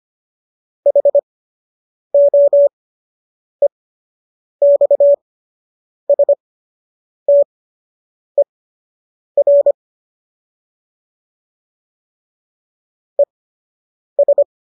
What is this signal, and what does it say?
Morse code, heard as a keyed tone, about 25 wpm, spaced as for 8 wpm, 570 Hz. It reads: HOEXSTER ES